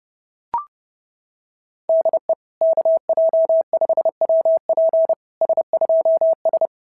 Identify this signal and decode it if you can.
Morse code, heard as a keyed tone, about 30 wpm, 690 Hz. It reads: DE KJ5WP S2S
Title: SOTA QSO Practice 30wpm